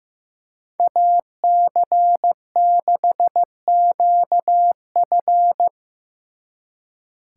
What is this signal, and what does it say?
Morse code, heard as a keyed tone, about 17 wpm, 700 Hz. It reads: AC6QF
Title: Speed Racing - USA 2x2 Call Signs 15wpm